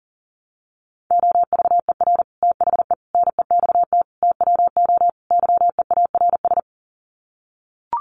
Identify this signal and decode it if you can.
Morse code, heard as a keyed tone, about 40 wpm, 700 Hz. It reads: OVER THE NEXT TWO YEARS
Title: Sets of 5 Words Encore2 40wpm